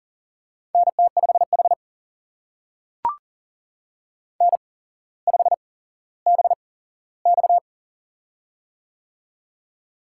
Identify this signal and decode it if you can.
Morse code, heard as a keyed tone, about 38 wpm, spaced as for 17 wpm, 700 Hz. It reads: NT5H N5BX
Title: USA Call Signs - 4 Characters - 8x Character Spacing - 40wpm